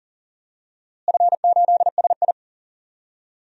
Morse code, heard as a keyed tone, about 40 wpm, 700 Hz. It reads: F8SI
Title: DXCC-100 Call Signs - 4 Characters - Repeated 3x - 40wpm